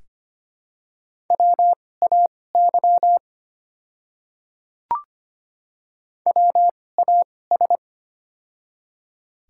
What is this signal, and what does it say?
Morse code, heard as a keyed tone, about 25 wpm, spaced as for 18 wpm, 700 Hz. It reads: WAY WAS
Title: 2x Character Spacing - Top 200 Words - 25wpm